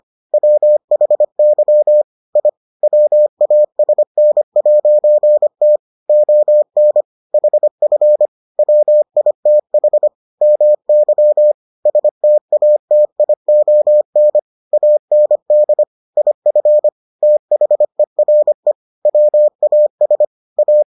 Morse code, heard as a keyed tone, about 25 wpm, 600 Hz. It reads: WHY I WASN'T ON HF WITH MY STATION AND IF THERE WAS A